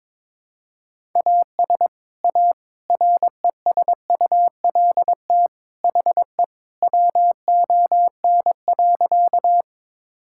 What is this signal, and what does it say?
Morse code, heard as a keyed tone, about 22 wpm, 700 Hz. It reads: AS A RESULT HE WON.
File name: practice / 22wpm / Sentences with 3-Word Phrases - 22wpm